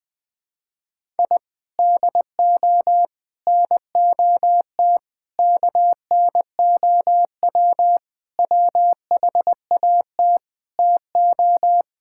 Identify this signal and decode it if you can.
Morse code, heard as a keyed tone, about 21 wpm, 700 Hz. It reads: I DO NOT KNOW WHAT TO
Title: Sentences with 5-Word Phrases - Spoken Beginning - 20wpm